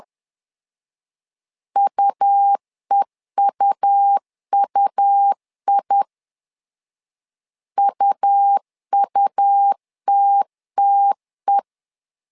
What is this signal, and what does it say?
Morse code, heard as a keyed tone, about 10 wpm, 800 Hz. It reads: UEUUI UUTTE
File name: LEZIONI / 2 UHDN / 2 UH 10wpm 1714